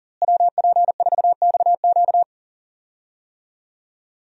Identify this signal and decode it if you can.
Morse code, heard as a keyed tone, about 40 wpm, 700 Hz. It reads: WP4XQ
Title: USA Call Signs - 5 Characters - Repeated 2x - 40wpm